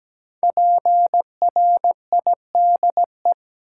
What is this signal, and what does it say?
Morse code, heard as a keyed tone, about 17 wpm, 700 Hz. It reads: PRIDE
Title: Five Letter Words - 17wpm